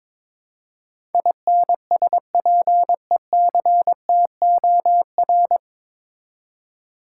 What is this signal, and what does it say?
Morse code, heard as a keyed tone, about 22 wpm, 700 Hz. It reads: INSPECTOR
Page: Morse Code Ninja practice 22wpm